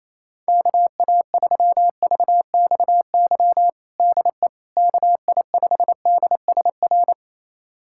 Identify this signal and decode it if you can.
Morse code, heard as a keyed tone, about 28 wpm, 700 Hz. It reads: KA3VXY DE KI5DSR
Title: USA Callsign DE Callsign 28wpm